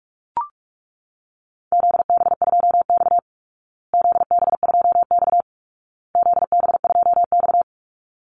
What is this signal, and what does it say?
Morse code, heard as a keyed tone, about 44 wpm, 700 Hz. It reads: ZB2X ZB2X ZB2X
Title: Call Signs Repeated 2x - 45wpm